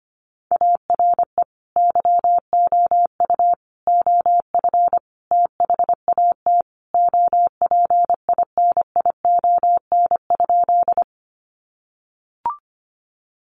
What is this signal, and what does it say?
Morse code, heard as a keyed tone, about 25 wpm, 700 Hz. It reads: ARE YOU OF THAT OPINION?
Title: Sentences from Top 1200 Words 25wpm